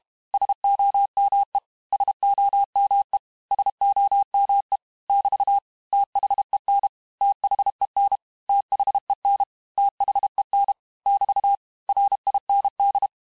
Morse code, heard as a keyed tone, about 32 wpm, 800 Hz. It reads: SOME SOME SOME <BT> THEN THEN THEN THEN <BT> RIND
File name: cours-du-mercredi / audios / cours-868-teste-MOTS-vitesse-30mpm